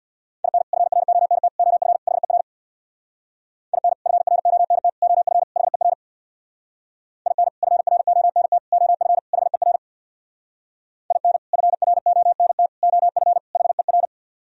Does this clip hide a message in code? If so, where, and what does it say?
Yes: Morse code from 0.4 to 14.1 s, heard as a keyed tone, about 64 wpm, 700 Hz. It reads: IN FRONT OF HER IN FRONT OF HER IN FRONT OF HER IN FRONT OF HER